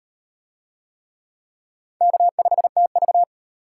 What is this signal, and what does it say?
Morse code, heard as a keyed tone, about 38 wpm, 700 Hz. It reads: K5TV